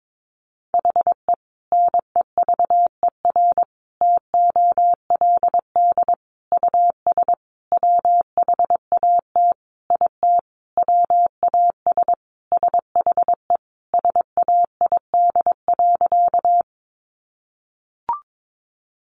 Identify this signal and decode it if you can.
Morse code, heard as a keyed tone, about 22 wpm, 700 Hz. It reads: HE NEVER TOLD US WHAT IT WAS SHE SAID.
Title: Sentences from Top 400 Words 22wpm